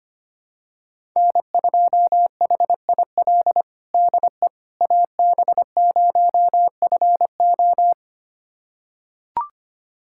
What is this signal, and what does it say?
Morse code, heard as a keyed tone, about 25 wpm, 700 Hz. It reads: N2HIL DE AB0FO
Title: USA Callsign DE Callsign - 5 Characters - 25wpm